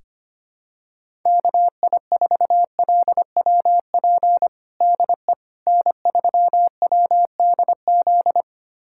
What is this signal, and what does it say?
Morse code, heard as a keyed tone, about 25 wpm, 700 Hz. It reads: KI4LWP DE N3WDZ
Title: USA Callsign DE Callsign 25wpm